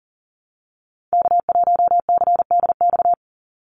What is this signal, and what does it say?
Morse code, heard as a keyed tone, about 40 wpm, 700 Hz. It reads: K1CDX
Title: Call Signs - 40wpm